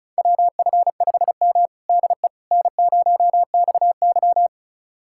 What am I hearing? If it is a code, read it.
Morse code, heard as a keyed tone, about 34 wpm, 700 Hz. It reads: WF5M DE N0XY
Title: USA Callsign DE Callsign - 4 Characters - 35wpm